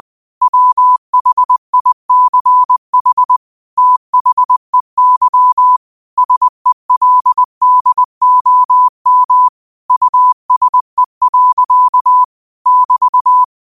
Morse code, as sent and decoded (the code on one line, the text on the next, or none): .-- .... .. -.-. .... / - .... -.-- / ... .-.. -.. --- -- / ..- ... .-.-.- / -...-
WHICH THEY SELDOM USE. <BT>